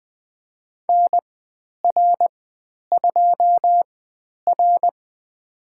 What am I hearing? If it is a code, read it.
Morse code, heard as a keyed tone, about 20 wpm, spaced as for 10 wpm, 700 Hz. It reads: NR2R
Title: 102 Lesson 17g - Letter u - Call Signs 20wpm